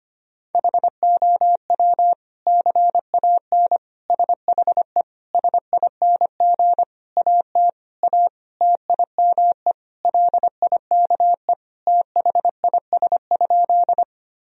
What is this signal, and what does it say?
Morse code, heard as a keyed tone, about 25 wpm, 700 Hz. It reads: HOW CAN SHE SING AT A TIME LIKE THIS?
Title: Sentences from Top 1000 Words - Review Part 2 - 25wpm